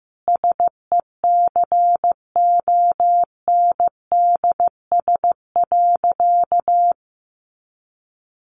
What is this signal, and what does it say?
Morse code, heard as a keyed tone, about 15 wpm, 700 Hz. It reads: SECONDS.
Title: A Sound Of Thunder 15wpm